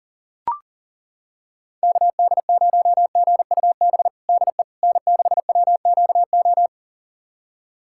Morse code, heard as a keyed tone, about 40 wpm, 700 Hz. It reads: KD0GUB DE N6WQO
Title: USA Callsign DE Callsign 40wpm